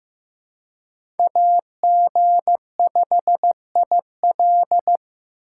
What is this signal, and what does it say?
Morse code, heard as a keyed tone, about 18 wpm, 700 Hz. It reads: AG5IL
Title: Speed Racing - USA 2x2 Call Signs 15wpm